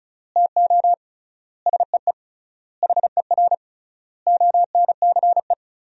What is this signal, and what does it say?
Morse code, heard as a keyed tone, about 36 wpm, 700 Hz. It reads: TO SEE HER ONCE
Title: 3x Word-Spacing - Sentences from Top 500 Words - 35wpm